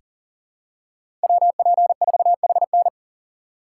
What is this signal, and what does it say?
Morse code, heard as a keyed tone, about 40 wpm, 700 Hz. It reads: WP4HN
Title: USA Call Signs - 5 Characters - 4x Character Spacing - 40wpm